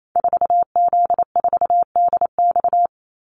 Morse code, heard as a keyed tone, about 28 wpm, 700 Hz. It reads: 4Z4DX
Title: Call Signs - 28wpm